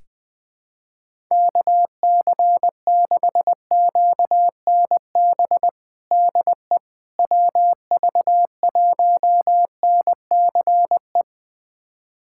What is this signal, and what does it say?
Morse code, heard as a keyed tone, about 20 wpm, 700 Hz. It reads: KC6QNB DE WV1NCE
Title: USA Callsign DE Callsign - 6 Characters - 20wpm